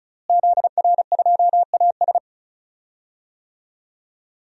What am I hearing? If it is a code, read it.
Morse code, heard as a keyed tone, about 34 wpm, 700 Hz. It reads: ZR2AS
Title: DXCC-100 Call Signs - 5 Characters - Repeated 3x - 35wpm